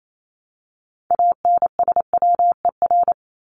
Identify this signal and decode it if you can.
Morse code, heard as a keyed tone, about 28 wpm, 700 Hz. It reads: ANSWER
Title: Top 500 Words 28wpm